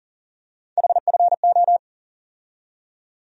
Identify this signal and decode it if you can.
Morse code, heard as a keyed tone, about 40 wpm, 700 Hz. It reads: HFO
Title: USA Call Sign Postfixes - 3 Letters 40wpm